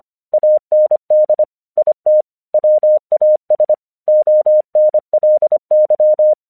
Morse code, heard as a keyed tone, about 25 wpm, 610 Hz. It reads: AND IT WAS ONLY
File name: 1 English 25wpm / AQC18